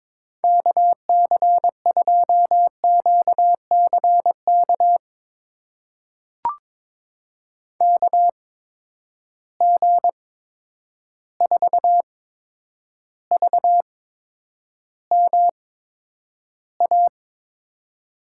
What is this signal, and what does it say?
Morse code, heard as a keyed tone, about 22 wpm, spaced as for 9 wpm, 700 Hz. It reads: KC2QCK KG4VMA
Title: USA Call Signs - 6 Characters - 8x Character Spacing - 22wpm